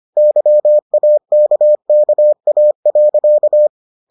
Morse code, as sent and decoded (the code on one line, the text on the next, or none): -.-- .- -.- -.- .- .-.-.-
YAKKA.